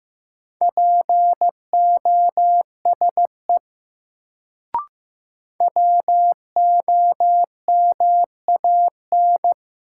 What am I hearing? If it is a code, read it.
Morse code, heard as a keyed tone, about 16 wpm, 700 Hz. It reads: POSE WOMAN
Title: Sentences from Top 1000 Words - Review Part 2 - New Word Drill - 15wpm